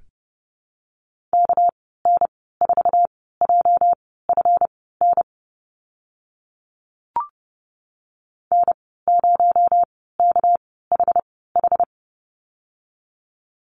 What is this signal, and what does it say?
Morse code, heard as a keyed tone, about 30 wpm, spaced as for 17 wpm, 700 Hz. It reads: KN4JFN N0KHH
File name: Lesson 37g - Letter x - Call Signs 30wpm